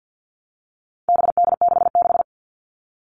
Morse code, heard as a keyed tone, about 50 wpm, 700 Hz. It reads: BD66
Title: All Cut Numbers - 4-Digit 50wpm